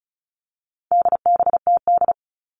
Morse code, heard as a keyed tone, about 34 wpm, 700 Hz. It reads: DBTD
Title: All Cut Numbers - 4-Digit 35wpm